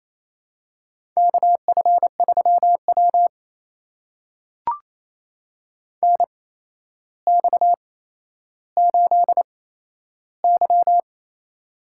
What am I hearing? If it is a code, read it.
Morse code, heard as a keyed tone, about 28 wpm, spaced as for 12 wpm, 700 Hz. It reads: KF3W NX8Y
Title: USA Call Signs - 4 Characters - 8x Character Spacing - 28wpm